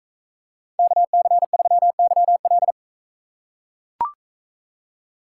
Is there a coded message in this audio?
Yes: Morse code from 0.8 to 2.7 s, heard as a keyed tone, about 43 wpm, 700 Hz. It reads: KC3YL